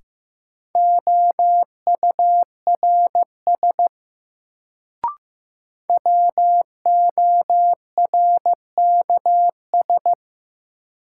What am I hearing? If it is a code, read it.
Morse code, heard as a keyed tone, about 15 wpm, 700 Hz. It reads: OURS WORKS
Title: Sentences from Top 1000 Words - Review Part 5 - New Word Drill - 15wpm